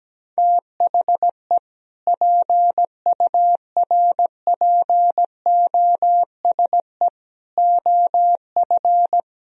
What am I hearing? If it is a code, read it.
Morse code, heard as a keyed tone, about 17 wpm, 700 Hz. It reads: THE PURPOSE OF
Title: Speed-Racing - Sets of 5 Words 17wpm